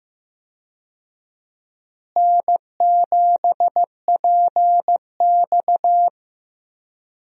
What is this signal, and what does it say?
Morse code, heard as a keyed tone, about 15 wpm, 700 Hz. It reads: N7PX
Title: USA Call Signs - 4 Characters - Repeated 2x - 15wpm